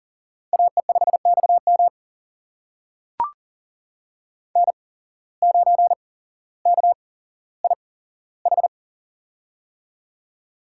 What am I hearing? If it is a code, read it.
Morse code, heard as a keyed tone, about 40 wpm, spaced as for 17 wpm, 700 Hz. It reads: AE5XM N9KIH
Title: USA Call Signs - 5 Characters - 8x Character Spacing - 40wpm